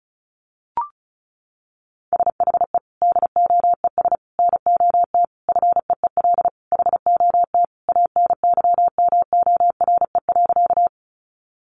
Morse code, heard as a keyed tone, about 34 wpm, 700 Hz. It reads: SHE DOES NOT FEEL HOT ANYMORE.
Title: Additional Easy Phrases 35wpm